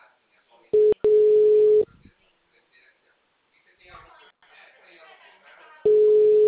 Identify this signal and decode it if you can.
Ringback tone (Europe, 425 Hz), with speech.
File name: out- -2056- - - .138